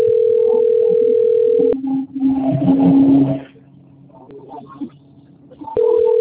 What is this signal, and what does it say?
Ringback tone (North America), 440 + 480 Hz.